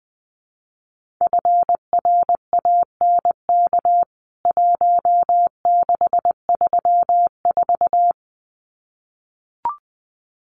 Morse code, heard as a keyed tone, about 20 wpm, 700 Hz. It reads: FRANK 1634
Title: CWT Exchange 20wpm